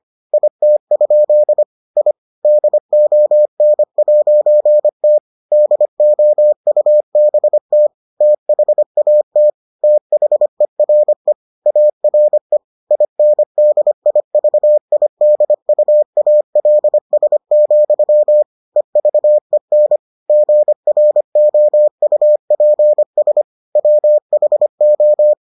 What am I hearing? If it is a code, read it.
Morse code, heard as a keyed tone, about 25 wpm, 600 Hz. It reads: IT? I DON'T DOUBT THAT THERE ARE INDIVIDUALS, EVEN GROUPS WHO